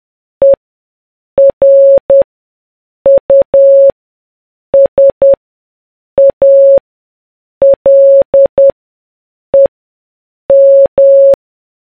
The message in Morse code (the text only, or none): ERUSALEM